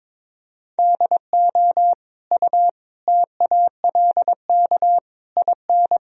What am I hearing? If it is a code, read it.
Morse code, heard as a keyed tone, about 21 wpm, 700 Hz. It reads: DO U TALK IN
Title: Sentences from Top 500 Words and Abbreviations 22wpm